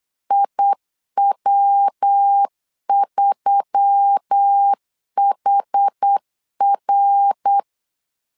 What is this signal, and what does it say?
Morse code, heard as a keyed tone, about 8 wpm, 800 Hz. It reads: IW3HR
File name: LEZIONI / 4 Q1S3 8wpm 3037